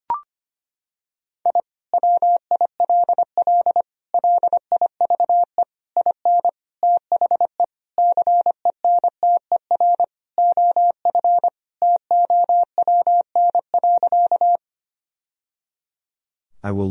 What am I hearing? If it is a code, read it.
Morse code, heard as a keyed tone, about 25 wpm, 700 Hz. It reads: I WILL LIVE IN THE CENTER OF TOWN.
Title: Sentences from Top 1000 Words - Review Part 2 - 25wpm